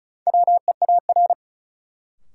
Morse code, heard as a keyed tone, about 36 wpm, 700 Hz. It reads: WEAR
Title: Sentences from Top 1500 Words - New Word Drill Alt - 35wpm